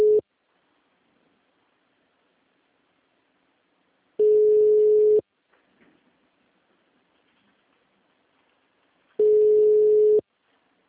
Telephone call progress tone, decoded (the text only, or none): ringback tone